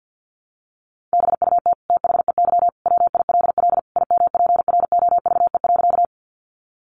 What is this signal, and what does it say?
Morse code, heard as a keyed tone, about 50 wpm, 700 Hz. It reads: BUT THEY WILL IMPROVE.
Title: Sentences from Top 800 Words 50wpm